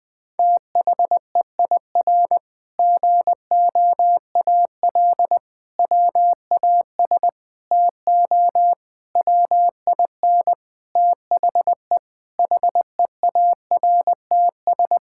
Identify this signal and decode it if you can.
Morse code, heard as a keyed tone, about 20 wpm, 700 Hz. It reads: THEIR GOAL WAS TO WIN THE HEARTS
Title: Sentences with 8-Word Phrases - 20wpm